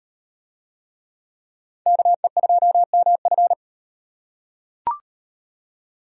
Morse code, heard as a keyed tone, about 37 wpm, 700 Hz. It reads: KE2MF